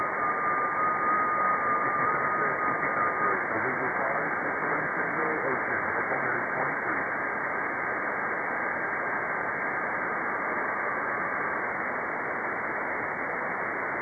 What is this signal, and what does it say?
FT8, with speech.